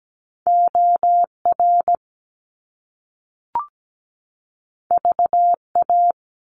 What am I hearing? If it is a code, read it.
Morse code, heard as a keyed tone, about 17 wpm, 700 Hz. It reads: OR VA
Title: US States 17wpm